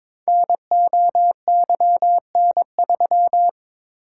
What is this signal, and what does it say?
Morse code, heard as a keyed tone, about 22 wpm, 700 Hz. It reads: NOYN3
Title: Code Groups - 5 Letter:Numbers - 22wpm